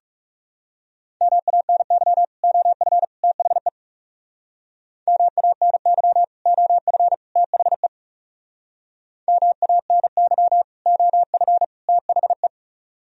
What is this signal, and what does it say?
Morse code, heard as a keyed tone, about 38 wpm, 700 Hz. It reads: MANY OF THE MANY OF THE MANY OF THE